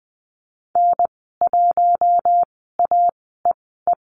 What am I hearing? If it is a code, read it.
Morse code, heard as a keyed tone, about 20 wpm, spaced as for 15 wpm, 700 Hz. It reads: N1AEE
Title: 048 Lesson 9g - Number 4 - Call Signs 20wpm